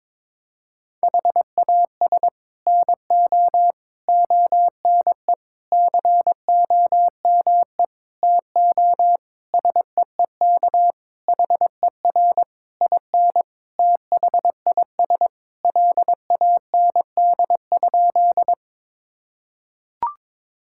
Morse code, heard as a keyed tone, about 21 wpm, 700 Hz. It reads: HAS NO ONE COME TO SEEK HER IN THIS LAND?
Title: Sentences from Top 1600 Words 22wpm